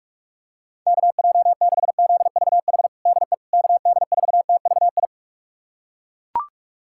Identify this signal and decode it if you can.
Morse code, heard as a keyed tone, about 42 wpm, 700 Hz. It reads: KJ6ZVH DE KD4TVI
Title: USA Callsign DE Callsign 45wpm